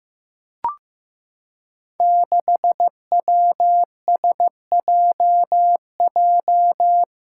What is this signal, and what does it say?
Morse code, heard as a keyed tone, about 15 wpm, 700 Hz. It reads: 6WSJJ
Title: Five Challenging Letter-Number Combinations 15wpm